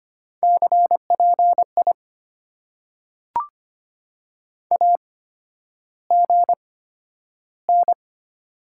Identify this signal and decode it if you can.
Morse code, heard as a keyed tone, about 25 wpm, spaced as for 11 wpm, 700 Hz. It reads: CPI AGN
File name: Top 90 QSO Elements - 8x Character Spacing - 25wpm